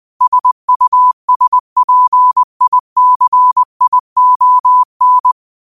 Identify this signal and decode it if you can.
Morse code, heard as a keyed tone, about 20 wpm, 1000 Hz. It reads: SUSPICION
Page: New quotes every day in morse code at 20 Words per minute.